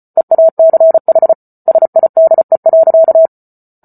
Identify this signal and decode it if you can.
Morse code, heard as a keyed tone, about 34 wpm, 650 Hz. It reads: EACH SIDE.